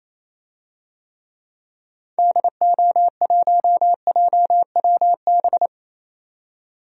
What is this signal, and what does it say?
Morse code, heard as a keyed tone, about 28 wpm, 700 Hz. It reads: DO1JWB
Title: DXCC-100 Call Signs - 6 Characters - 28wpm